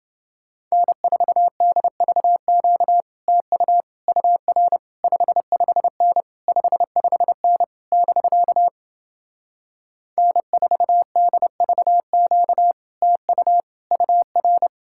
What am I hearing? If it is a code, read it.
Morse code, heard as a keyed tone, about 29 wpm, 700 Hz. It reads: N4DVQ TU UR 55N 55N <BK> N4DVQ TU UR